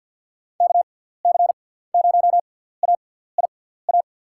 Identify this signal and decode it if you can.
Morse code, heard as a keyed tone, about 47 wpm, spaced as for 17 wpm, 700 Hz. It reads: KC0AIA
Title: USA Call Signs - 6x Character Spacing - 50wpm